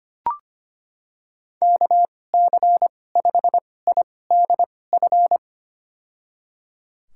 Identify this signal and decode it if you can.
Morse code, heard as a keyed tone, about 25 wpm, spaced as for 18 wpm, 700 Hz. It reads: KC5IDF
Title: USA Call Signs - 6 Characters - 2x Character Spacing - 25wpm